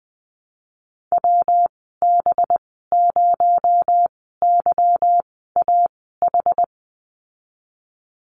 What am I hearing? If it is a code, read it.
Morse code, heard as a keyed tone, about 20 wpm, spaced as for 14 wpm, 700 Hz. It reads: WB0YAH
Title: 245 Lesson 39g - Letter z - Call Signs 20wpm